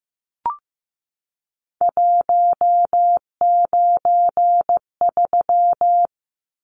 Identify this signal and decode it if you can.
Morse code, heard as a keyed tone, about 15 wpm, 700 Hz. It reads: 193
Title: Numbers - 3-Digit 15wpm